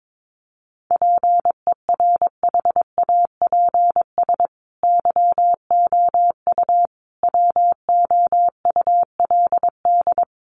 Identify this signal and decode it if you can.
Morse code, heard as a keyed tone, about 22 wpm, 700 Hz. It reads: PERHAPS YOU WOULD
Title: Sentences from Top 500 Words 22wpm